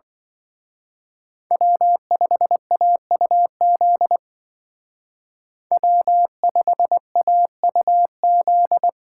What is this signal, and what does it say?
Morse code, heard as a keyed tone, about 23 wpm, 700 Hz. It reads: W5AUZ W5AUZ
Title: Speed Racing - USA Call Signs 20wpm